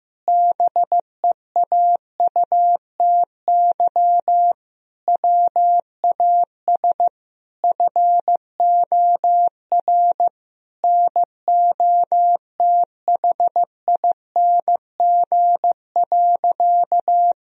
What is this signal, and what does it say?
Morse code, heard as a keyed tone, about 15 wpm, 700 Hz. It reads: BEAUTY WAS FOR NOTHING.
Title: Sentences from Top 1000 Words - Review Part 2 - 15wpm